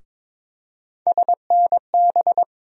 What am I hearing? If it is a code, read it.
Morse code, heard as a keyed tone, about 23 wpm, 700 Hz. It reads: SNB
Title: USA Call Sign Postfixes - 3 Letters 22wpm